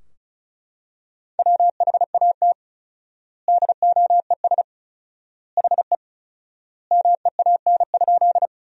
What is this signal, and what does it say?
Morse code, heard as a keyed tone, about 36 wpm, 700 Hz. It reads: WHAT DOES HE MEAN?
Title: 4x Word-Spacing - Sentences from Top 200 Words - 35wpm